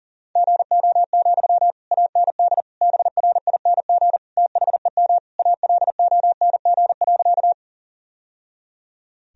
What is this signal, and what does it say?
Morse code, heard as a keyed tone, about 40 wpm, 700 Hz. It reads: GO, AND BRING THEM ALONG.
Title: Sentences from Top 1000 Words - Review Part 3 - 40wpm